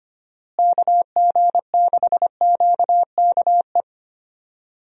Morse code, heard as a keyed tone, about 25 wpm, 700 Hz. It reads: KG6QKE
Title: USA Call Signs - 6 Characters - Repeated 2x - 25wpm